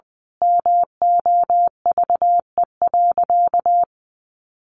Morse code, heard as a keyed tone, about 19 wpm, 700 Hz. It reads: MOVE.
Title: Sentences from Top 1400 Words 20wpm